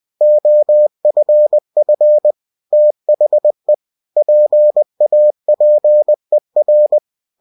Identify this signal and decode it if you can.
Morse code, heard as a keyed tone, about 20 wpm, 600 Hz. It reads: OFF THE PAPER
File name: cw-books / winnie-the-pooh / winnie-20wpm-page-0037